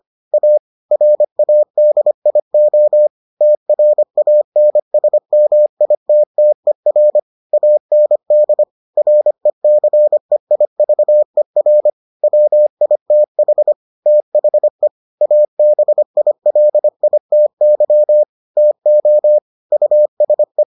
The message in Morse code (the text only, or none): A RADIO TRANSMITTER AND RECEIVER WITH THE ABILITY TO USE